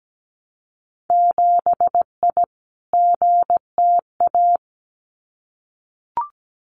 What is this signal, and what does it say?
Morse code, heard as a keyed tone, about 17 wpm, 700 Hz. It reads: 7I GTA
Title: Winter Field Day Exchanges Encore 17wpm